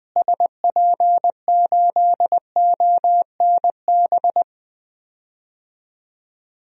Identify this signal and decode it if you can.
Morse code, heard as a keyed tone, about 20 wpm, 700 Hz. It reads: SP8ONB
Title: DXCC-100 Call Signs - 6 Characters - Repeated 3x - 20wpm